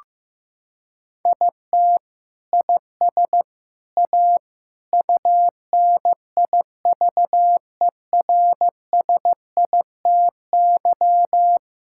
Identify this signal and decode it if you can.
Morse code, heard as a keyed tone, about 15 wpm, 700 Hz. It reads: IT IS A UNIVERSITY
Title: Sentences from Top 1000 Words - Review Part 5 - 15wpm